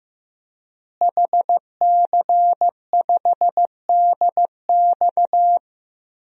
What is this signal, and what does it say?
Morse code, heard as a keyed tone, about 15 wpm, 700 Hz. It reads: HC5DX
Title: DXCC-100 Call Signs - 5 Characters - Repeated 3x - 15wpm